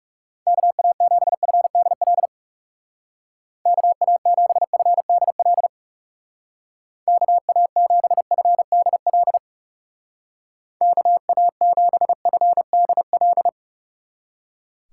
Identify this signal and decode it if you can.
Morse code, heard as a keyed tone, about 38 wpm, 700 Hz. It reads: KA7FDL KA7FDL KA7FDL KA7FDL